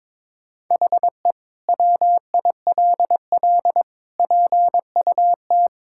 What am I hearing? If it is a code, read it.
Morse code, heard as a keyed tone, about 22 wpm, 700 Hz. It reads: HE WILL PUT
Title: Sentences from Top 1000 Words - Review Part 5 - 22wpm